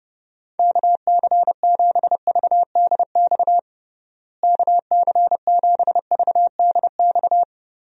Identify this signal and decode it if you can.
Morse code, heard as a keyed tone, about 30 wpm, 700 Hz. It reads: KC7VDX KC7VDX
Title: USA Call Signs - 6 Characters - Repeated 3x - 30wpm